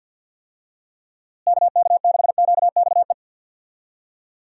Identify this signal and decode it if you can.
Morse code, heard as a keyed tone, about 50 wpm, 700 Hz. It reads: KK6QXE
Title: USA Call Signs - 6 Characters - Repeated 2x - 50wpm